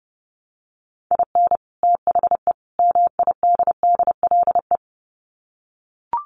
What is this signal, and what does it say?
Morse code, heard as a keyed tone, about 30 wpm, 700 Hz. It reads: IN THE MIDDLE
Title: Sets of 3 Words 30wpm